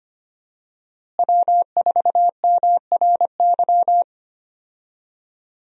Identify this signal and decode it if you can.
Morse code, heard as a keyed tone, about 25 wpm, 700 Hz. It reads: W4MRY
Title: USA Call Signs - 5 Characters - Repeated 2x - 25wpm